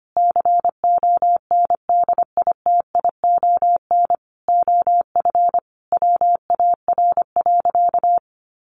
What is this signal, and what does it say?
Morse code, heard as a keyed tone, about 25 wpm, 700 Hz. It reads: CONDITION OF WAR.
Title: Sentences from Top 1500 Words 25wpm